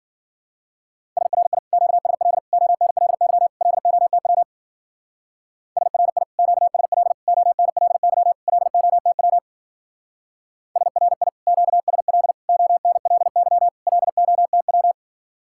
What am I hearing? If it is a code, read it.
Morse code, heard as a keyed tone, about 48 wpm, 700 Hz. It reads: SRI QSL ONLY LOTW SRI QSL ONLY LOTW SRI QSL ONLY LOTW
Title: Speed-Racing Common QSO Phrases - 40wpm